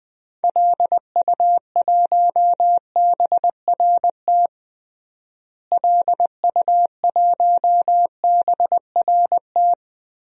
Morse code, heard as a keyed tone, about 20 wpm, 700 Hz. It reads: LU1BRT LU1BRT
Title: DXCC-100 Call Signs - 6 Characters - Repeated 3x - 20wpm